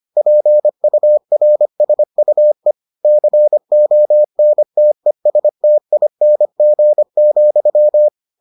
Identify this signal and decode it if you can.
Morse code, heard as a keyed tone, about 25 wpm, 600 Hz. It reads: PURSUE CONTESTING,